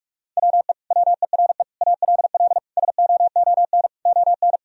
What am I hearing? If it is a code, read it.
Morse code, heard as a keyed tone, about 46 wpm, 700 Hz. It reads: WE WERE ALL SOON ON
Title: Sentences from Top 1000 Words - Review Part 4 - 45wpm